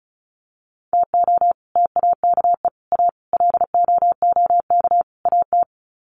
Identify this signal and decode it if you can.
Morse code, heard as a keyed tone, about 34 wpm, 700 Hz. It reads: TO TAKE A LOOK AT
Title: Sets of 5 Words 35wpm